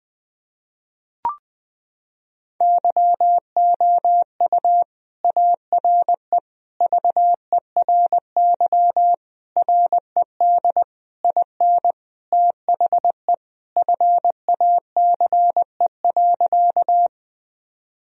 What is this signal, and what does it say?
Morse code, heard as a keyed tone, about 20 wpm, 700 Hz. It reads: YOU ARE VERY RED IN THE FACE.
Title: Sentences from Top 1000 Words - Review Part 2 - 20wpm